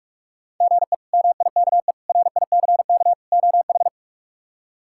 Morse code, heard as a keyed tone, about 42 wpm, 700 Hz. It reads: GE MIKE RICK OH
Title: K1USN SST QSO Practice - 45wpm